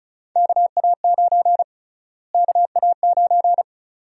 Morse code, heard as a keyed tone, about 34 wpm, 700 Hz. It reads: KA9 KA9
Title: USA Call Sign Prefixes - Repeated 2x - 35wpm